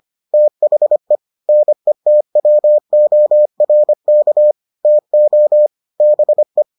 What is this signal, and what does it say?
Morse code, heard as a keyed tone, about 25 wpm, 600 Hz. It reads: THE NETWORK TO BE